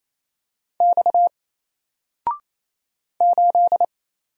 Morse code, heard as a keyed tone, about 27 wpm, 700 Hz. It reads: X 8
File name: Single-Letter-Number-28wpm